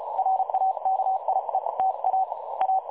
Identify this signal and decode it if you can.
Morse code, heard as a keyed tone, about 32 wpm, 760 Hz, with noise. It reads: RAKIETA A
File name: dlazaawansowanych / nagraniazpasma / websdr recording 3565.0kHz